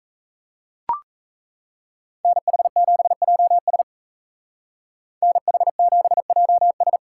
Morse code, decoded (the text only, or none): NH7JS NH7JS